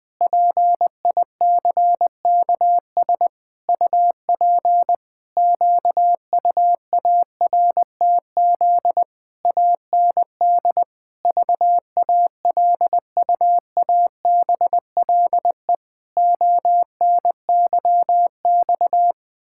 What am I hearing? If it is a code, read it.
Morse code, heard as a keyed tone, about 21 wpm, 700 Hz. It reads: PICKS UP QUARTZ AND VALUABLE ONYX
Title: Pangram Challenge v2 - 20wpm